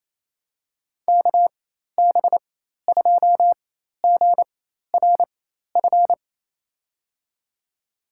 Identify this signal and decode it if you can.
Morse code, heard as a keyed tone, about 27 wpm, spaced as for 13 wpm, 700 Hz. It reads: KB2GRF
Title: USA Call Signs - 6 Characters - 4x Character Spacing - 28wpm